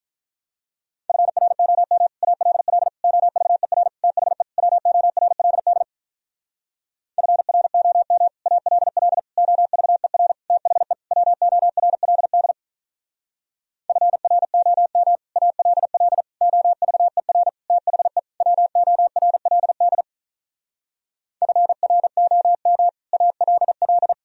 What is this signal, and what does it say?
Morse code, heard as a keyed tone, about 46 wpm, 700 Hz. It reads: FROM ALL OVER THE WORLD FROM ALL OVER THE WORLD FROM ALL OVER THE WORLD FROM ALL